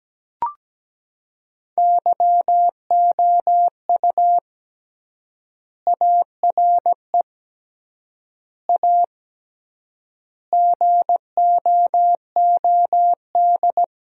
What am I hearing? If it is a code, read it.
Morse code, heard as a keyed tone, about 17 wpm, 700 Hz. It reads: YOU ARE A GOOD
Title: 3x Word-Spacing - Sentences from Top 400 Words - 17wpm